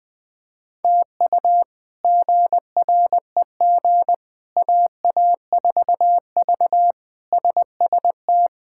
Morse code, heard as a keyed tone, about 20 wpm, 700 Hz. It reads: TU GREG AA4V SST
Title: K1USN SST QSO Practice - 20wpm